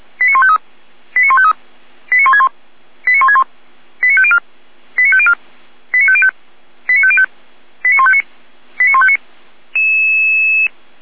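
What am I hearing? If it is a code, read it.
Five-tone selective call (ZVEI), heard as quick runs of five tones, tones of about 70 ms. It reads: 89153 89153 89161 89161 89504 89504 89505 89505 89159 89159
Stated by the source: Received on an SDR, ZVEI five-tone code 89099.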